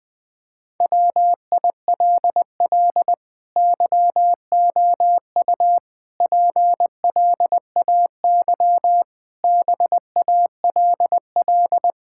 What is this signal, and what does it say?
Morse code, heard as a keyed tone, about 20 wpm, 700 Hz. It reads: WILL YOU PLAY BALL
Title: Sentences from Top 1000 Words - Review Part 6 - 20wpm